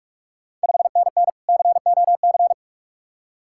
Morse code, heard as a keyed tone, about 46 wpm, 690 Hz. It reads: 5NN COC
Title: Ontario QSO Party Exchange 45wpm